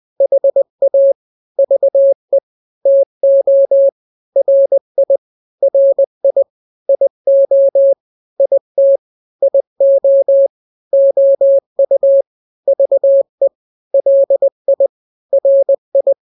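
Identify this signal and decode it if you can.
Morse code, heard as a keyed tone, about 20 wpm, 550 Hz. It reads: HA VE TO RI RI IO IT IO OU VE LI RI
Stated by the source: Morse drill of two-letter n-grams at 20/18 WPM (Farnsworth), from KK7RR.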